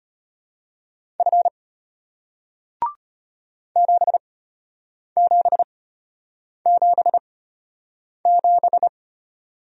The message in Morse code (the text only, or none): F 7777